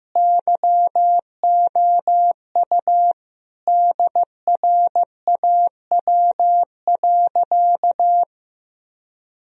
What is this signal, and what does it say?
Morse code, heard as a keyed tone, about 16 wpm, 700 Hz. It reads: YOU DRAW.
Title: Sentences from Top 1000 Words - Review Part 1 - 15wpm